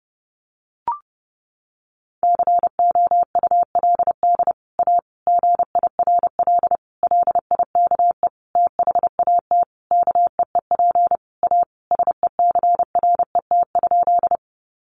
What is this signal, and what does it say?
Morse code, heard as a keyed tone, about 30 wpm, 700 Hz. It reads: COULD A GIRL LIKE THAT KEEP A SECRET?
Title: Sentences from Top 1200 Words 30wpm